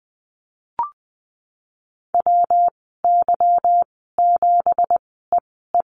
Morse code, heard as a keyed tone, about 20 wpm, spaced as for 15 wpm, 700 Hz. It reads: WY7EE
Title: 167 Lesson 27g - Number 7 - Call Signs 20wpm